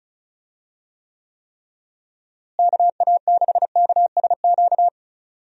Morse code, heard as a keyed tone, about 34 wpm, 700 Hz. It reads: KA6KSQ
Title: USA Call Signs - 6 Characters - Repeated 3x - 35wpm